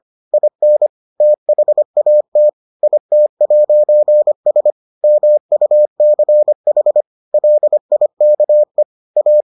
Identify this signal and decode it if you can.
Morse code, heard as a keyed tone, about 25 wpm, 600 Hz. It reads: IN THAT IT'S MUCH LIKE A